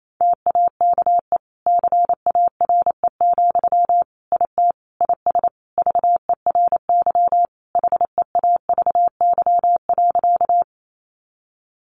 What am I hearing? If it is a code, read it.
Morse code, heard as a keyed tone, about 28 wpm, 700 Hz. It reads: TAKE CARE, IT IS VERY HEAVY.
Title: Sentences from Top 1100 Words 28wpm